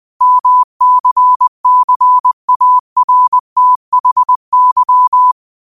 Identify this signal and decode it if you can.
Morse code, heard as a keyed tone, about 20 wpm, 1000 Hz. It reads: MCCARTHY
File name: qotdfiles / 20wpm